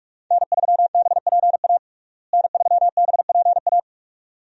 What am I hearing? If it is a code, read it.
Morse code, heard as a keyed tone, about 46 wpm, 700 Hz. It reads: N3BPA N3BPA
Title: USA Call Signs - 5 Characters - Repeated 2x - 45wpm